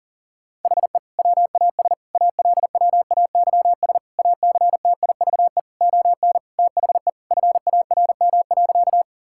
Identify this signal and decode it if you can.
Morse code, heard as a keyed tone, about 40 wpm, 700 Hz. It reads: HE WAS ALWAYS ACTIVE ON THE FARM.
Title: Sentences from Top 1000 Words - Review Part 6 - 40wpm